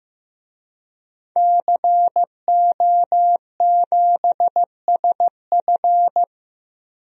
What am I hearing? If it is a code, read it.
Morse code, heard as a keyed tone, about 15 wpm, 700 Hz. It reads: CO7SF
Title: DXCC-100 Call Signs - 5 Characters - Repeated 2x - 15wpm